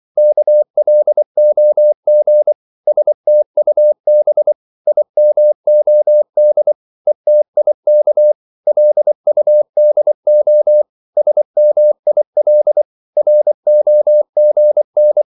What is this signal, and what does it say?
Morse code, heard as a keyed tone, about 24 wpm, 600 Hz. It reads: KLOG STUB IMOD ETIK LUDO SMIL ROGN
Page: Korte danske ord 24wpm